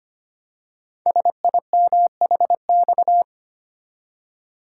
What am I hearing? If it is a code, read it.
Morse code, heard as a keyed tone, about 25 wpm, 700 Hz. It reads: SIMHX
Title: Code Groups - 5 Letters - 25wpm